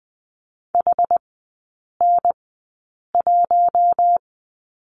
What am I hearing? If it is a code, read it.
Morse code, heard as a keyed tone, about 20 wpm, spaced as for 8 wpm, 700 Hz. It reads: HN1
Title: 067 Lesson 12e - Letter d - Three Characters 20wpm